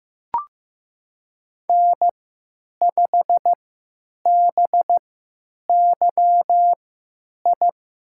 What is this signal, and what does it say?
Morse code, heard as a keyed tone, about 15 wpm, spaced as for 9 wpm, 700 Hz. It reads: N5BYI
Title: USA Call Signs - 5 Characters - 3x Character Spacing - 15wpm